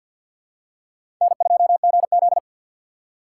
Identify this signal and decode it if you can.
Morse code, heard as a keyed tone, about 52 wpm, 700 Hz. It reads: N2GZ
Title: USA Call Signs - 4 Characters - 3x Character Spacing - 50wpm